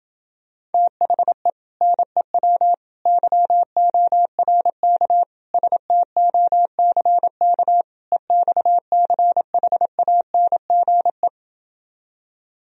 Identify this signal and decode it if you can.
Morse code, heard as a keyed tone, about 29 wpm, 700 Hz. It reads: THE NEW YORK STOCK EXCHANGE